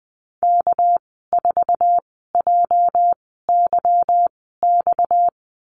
Morse code, heard as a keyed tone, about 20 wpm, spaced as for 15 wpm, 700 Hz. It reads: K4JYX
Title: 245 Lesson 39g - Letter z - Call Signs 20wpm